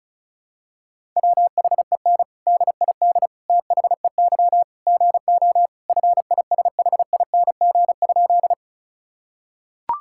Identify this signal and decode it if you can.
Morse code, heard as a keyed tone, about 36 wpm, 700 Hz. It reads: WHEN DID THEY GO FISHING?
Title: Sentences from Top 1000 Words - Review Part 5 - 35wpm